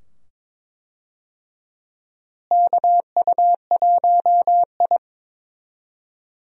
Morse code, heard as a keyed tone, about 22 wpm, 700 Hz. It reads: KU1I
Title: USA Call Signs - 4 Characters - Repeated 2x - 22wpm